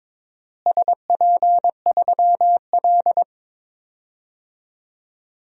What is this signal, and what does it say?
Morse code, heard as a keyed tone, about 22 wpm, 700 Hz. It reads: SP3L
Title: DXCC-100 Call Signs - 4 Characters - Repeated 2x - 22wpm